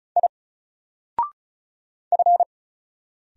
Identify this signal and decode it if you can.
Morse code, heard as a keyed tone, about 34 wpm, 700 Hz. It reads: I F
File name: Single-Letter-Number-35wpm